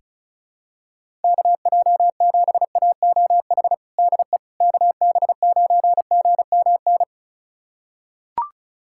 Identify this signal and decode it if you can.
Morse code, heard as a keyed tone, about 34 wpm, 700 Hz. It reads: KJ7AOH DE KB9GMN
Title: USA Callsign DE Callsign - 6 Characters - 35wpm